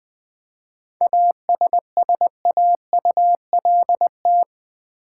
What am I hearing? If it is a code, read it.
Morse code, heard as a keyed tone, about 20 wpm, 700 Hz. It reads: ASSAULT
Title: 7 Letter Words - First 3 Letters Given - 20wpm